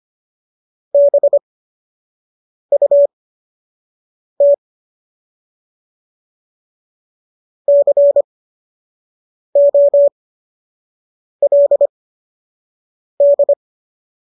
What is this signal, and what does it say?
Morse code, heard as a keyed tone, about 25 wpm, spaced as for 6 wpm, 570 Hz. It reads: BUT COLD
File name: qso002 06